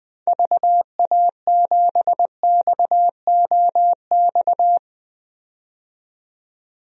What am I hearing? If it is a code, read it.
Morse code, heard as a keyed tone, about 20 wpm, 700 Hz. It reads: VA7XOX
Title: DXCC-100 Call Signs - 6 Characters - 20wpm